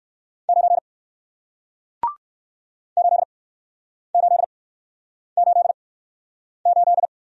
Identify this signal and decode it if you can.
Morse code, heard as a keyed tone, about 61 wpm, spaced as for 9 wpm, 700 Hz. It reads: 9 8888